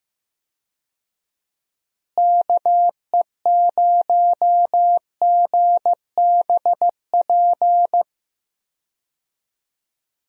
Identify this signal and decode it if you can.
Morse code, heard as a keyed tone, about 15 wpm, 700 Hz. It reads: KE0GBP
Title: USA Call Signs - 6 Characters - Repeated 3x - 15wpm